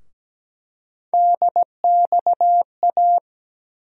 Morse code, heard as a keyed tone, about 17 wpm, 700 Hz. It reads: DXA
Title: USA Call Sign Postfixes - 3 Letters 17wpm